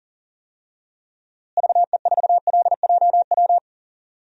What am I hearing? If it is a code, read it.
Morse code, heard as a keyed tone, about 40 wpm, 700 Hz. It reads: VE4LJW
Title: DXCC-100 Call Signs - 6 Characters - Repeated 2x - 40wpm